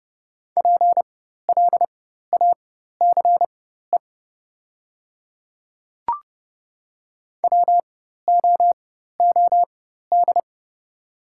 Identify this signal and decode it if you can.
Morse code, heard as a keyed tone, about 30 wpm, spaced as for 14 wpm, 700 Hz. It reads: PLACE WOOD
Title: 4x Character Spacing - Top 300 Words - 30wpm